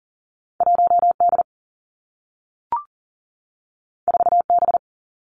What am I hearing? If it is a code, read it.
Morse code, heard as a keyed tone, about 39 wpm, 700 Hz. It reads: 1 D 4 B
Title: Two Challenging Letter-Number Combinations 40wpm